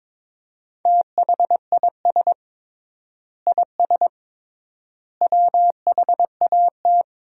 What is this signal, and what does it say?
Morse code, heard as a keyed tone, about 22 wpm, 700 Hz. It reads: THIS IS WHAT
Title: 3x Word-Spacing - Sentences from Top 300 Words - 22wpm